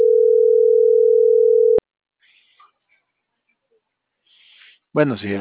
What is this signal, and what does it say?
Ringback tone (North America), 440 + 480 Hz.